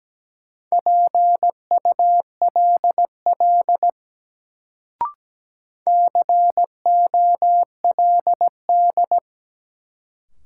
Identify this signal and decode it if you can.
Morse code, heard as a keyed tone, about 17 wpm, 700 Hz. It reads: PULL COLD
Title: Sentences from Top 400 Words - New Word Drill - 17wpm